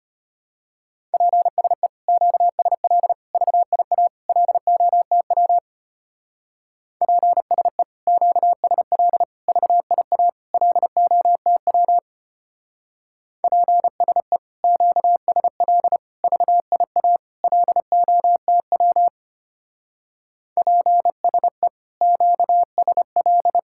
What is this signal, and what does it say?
Morse code, heard as a keyed tone, about 31 wpm, 700 Hz. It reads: PSE QSL VIA LOTW PSE QSL VIA LOTW PSE QSL VIA LOTW PSE QSL
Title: Speed-Racing Common QSO Phrases - 25wpm